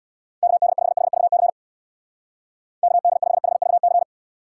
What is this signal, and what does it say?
Morse code, heard as a keyed tone, about 65 wpm, 700 Hz. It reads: KD5HVC KD5HVC